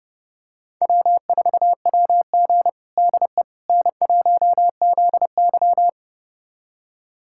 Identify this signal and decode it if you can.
Morse code, heard as a keyed tone, about 31 wpm, 700 Hz. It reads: W4WG DE N1ZY
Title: USA Callsign DE Callsign - 4 Characters - 30wpm